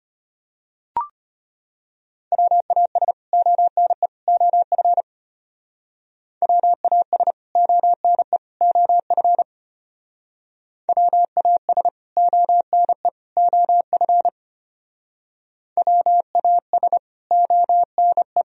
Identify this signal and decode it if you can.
Morse code, heard as a keyed tone, about 32 wpm, 700 Hz. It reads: WAS ONE OF WAS ONE OF WAS ONE OF WAS ONE